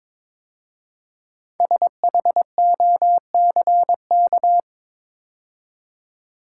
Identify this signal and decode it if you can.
Morse code, heard as a keyed tone, about 22 wpm, 700 Hz. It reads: SHOCK
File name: 5 Letter Words - First 3 Letters Given - 22wpm